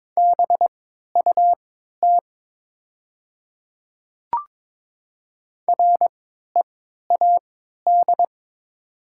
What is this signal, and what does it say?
Morse code, heard as a keyed tone, about 22 wpm, spaced as for 12 wpm, 690 Hz. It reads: BUT READ
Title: 3x Character Spacing - Top 300 Words - 22wpm